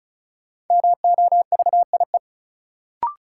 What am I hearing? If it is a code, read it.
Morse code, heard as a keyed tone, about 35 wpm, 700 Hz. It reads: MOVIE
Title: Sentences from Top 1000 Words - Review Part 5 - New Word Drill - 35wpm